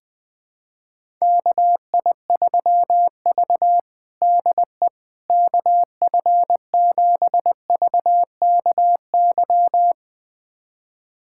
Morse code, heard as a keyed tone, about 21 wpm, 700 Hz. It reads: KI3V DE KF7VKY
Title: USA Callsign DE Callsign 20wpm